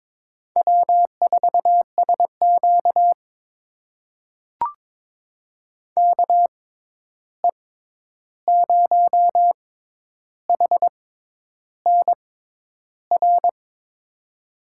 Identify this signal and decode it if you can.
Morse code, heard as a keyed tone, about 22 wpm, spaced as for 10 wpm, 700 Hz. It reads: W4SQ KE0HNR
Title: USA Call Signs - 6x Character Spacing - 22wpm